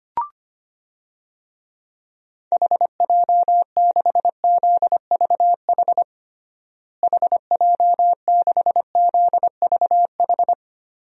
Morse code, heard as a keyed tone, about 25 wpm, 700 Hz. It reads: HJ6ZVH HJ6ZVH
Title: DXCC-100 Call Signs - 6 Characters - Repeated 3x - 25wpm